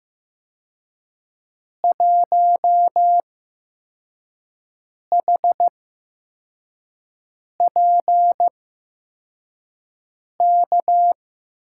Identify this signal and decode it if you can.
Morse code, heard as a keyed tone, about 15 wpm, spaced as for 4 wpm, 700 Hz. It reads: 1HPK
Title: USA Call Signs - 6 Characters - 8x Character Spacing - 15wpm